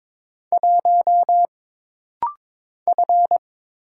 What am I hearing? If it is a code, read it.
Morse code, heard as a keyed tone, about 22 wpm, 700 Hz. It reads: 1 F
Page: This rapid-fire timing is geared to push your ICR (Instant Character Recognition) skills.